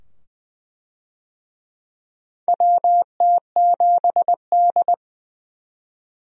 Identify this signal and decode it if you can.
Morse code, heard as a keyed tone, about 20 wpm, 700 Hz. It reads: WT7D
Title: USA Call Signs - 4 Characters - 20wpm